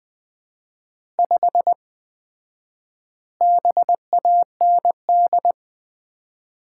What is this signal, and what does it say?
Morse code, heard as a keyed tone, about 20 wpm, 700 Hz. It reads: 5 BAND
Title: Common QSO Phrases - 4x Word-Spacing - 20wpm